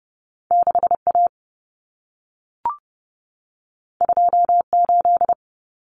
Morse code, heard as a keyed tone, about 30 wpm, 700 Hz. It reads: 6A 28
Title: Common Cut Numbers - 2-Digit 30wpm